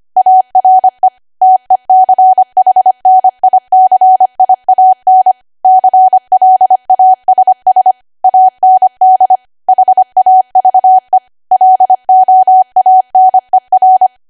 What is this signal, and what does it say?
Morse code, heard as a keyed tone, about 25 wpm, 750 Hz. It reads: ARE TECHNICIAN CLASS AND HAVE LOANER
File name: Archive / 25 WPM / 25WPM